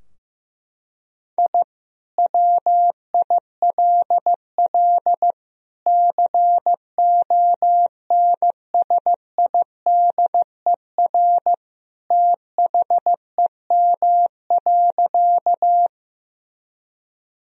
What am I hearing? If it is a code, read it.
Morse code, heard as a keyed tone, about 15 wpm, 700 Hz. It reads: I WILL CONSIDER THEM.
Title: Sentences from Top 1000 Words - Review Part 4 - 15wpm